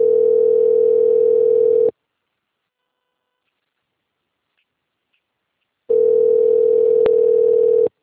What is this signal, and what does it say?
Ringback tone (North America), 440 + 480 Hz.